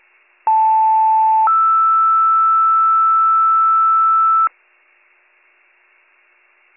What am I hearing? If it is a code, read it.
Two-tone page, heard as a one-second tone, then a three-second tone, about 880 then 1320 Hz. It is heard from 0.5 to 4.5 s.